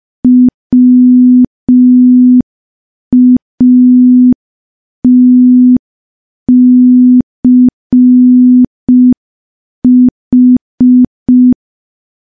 Morse code, heard as a keyed tone, about 5 wpm, 250 Hz. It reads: WATCH
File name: morse / morse-training-05WPM